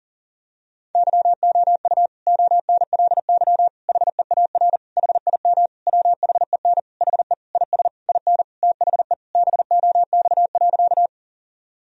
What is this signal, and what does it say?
Morse code, heard as a keyed tone, about 40 wpm, 700 Hz. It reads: YOU ONLY HEAR HIM WHEN HE IS IN THE BOX.
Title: Sentences from Top 1000 Words - Review Part 2 - 40wpm